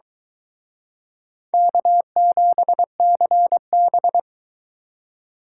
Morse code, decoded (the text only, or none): K7CB